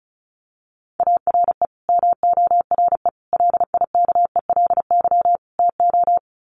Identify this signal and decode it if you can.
Morse code, heard as a keyed tone, about 34 wpm, 700 Hz. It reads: ARE MORE LIKELY TO
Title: Sets of 4 Words 35wpm